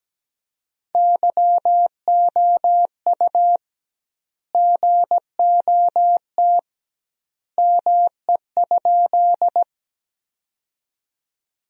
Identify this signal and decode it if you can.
Morse code, heard as a keyed tone, about 17 wpm, 700 Hz. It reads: YOU GOT ME?
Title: 2x Word-Spacing - Sentences from Top 300 Words - 17wpm